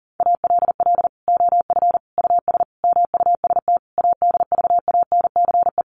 Morse code, heard as a keyed tone, about 40 wpm, 700 Hz. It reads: ALL OF US MUST ADVANCE
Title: Sentences from Top 1400 Words 40wpm